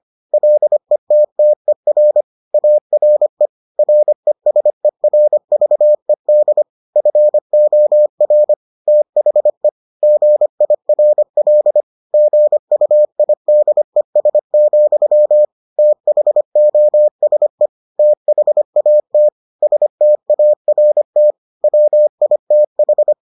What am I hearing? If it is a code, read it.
Morse code, heard as a keyed tone, about 25 wpm, 600 Hz. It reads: LETTER ARE RESERVED FOR THE GIRL GUIDES, THOSE THAT START WITH